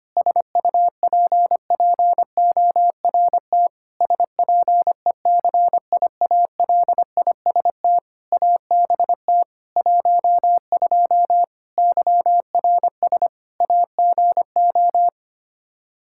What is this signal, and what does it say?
Morse code, heard as a keyed tone, about 25 wpm, 700 Hz. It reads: SUPPORT SPECIALIST ABT 12 YRS AGO
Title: Practice QSO - Exchange 3 - More Info - 25wpm